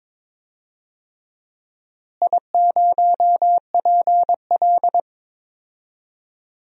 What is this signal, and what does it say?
Morse code, heard as a keyed tone, about 22 wpm, 700 Hz. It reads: I0PL
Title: DXCC-100 Call Signs - 4 Characters - Repeated 3x - 22wpm